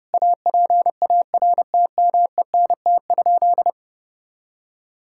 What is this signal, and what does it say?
Morse code, heard as a keyed tone, about 30 wpm, 700 Hz. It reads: APARTMENT?
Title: Sentences from Top 1000 Words - Review Part 6 - 30wpm